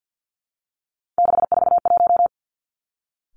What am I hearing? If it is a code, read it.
Morse code, heard as a keyed tone, about 50 wpm, 700 Hz. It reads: 641
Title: Three Challenging Letter-Number Combinations 50wpm